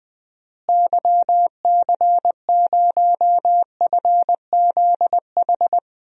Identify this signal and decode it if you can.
Morse code, heard as a keyed tone, about 20 wpm, 700 Hz. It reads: YC0FZH
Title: DXCC-100 Call Signs - 6 Characters - Repeated 2x - 20wpm